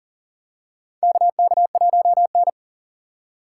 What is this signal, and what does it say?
Morse code, heard as a keyed tone, about 40 wpm, 700 Hz. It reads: KK1N
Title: USA Call Signs - 4 Characters - 4x Character Spacing - 40wpm